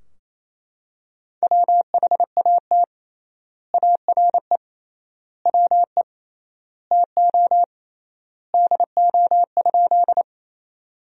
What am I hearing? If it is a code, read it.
Morse code, heard as a keyed tone, about 28 wpm, 700 Hz. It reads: WHAT ARE WE TO DO?
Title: 3x Word-Spacing - Sentences from Top 100 Words - 28wpm